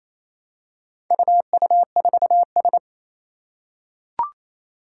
Morse code, heard as a keyed tone, about 27 wpm, 700 Hz. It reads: UU4S
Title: Four Challenging Letter-Number Combinations 28wpm